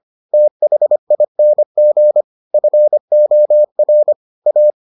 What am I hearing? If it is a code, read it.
Morse code, heard as a keyed tone, about 25 wpm, 600 Hz. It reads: THING FOR A